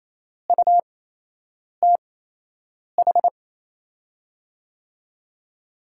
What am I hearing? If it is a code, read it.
Morse code, heard as a keyed tone, about 27 wpm, spaced as for 8 wpm, 700 Hz. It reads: UTH
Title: USA Call Signs - 6 Characters - 8x Character Spacing - 28wpm